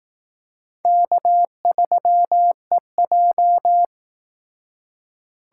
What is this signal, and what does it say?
Morse code, heard as a keyed tone, about 18 wpm, 700 Hz. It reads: K3EJ
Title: Speed Racing - USA 1x2 Call Signs 15wpm